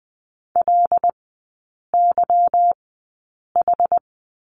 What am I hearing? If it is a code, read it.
Morse code, heard as a keyed tone, about 20 wpm, spaced as for 8 wpm, 700 Hz. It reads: LYH
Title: 166 Lesson 27e - Number 7 - Three Characters 20wpm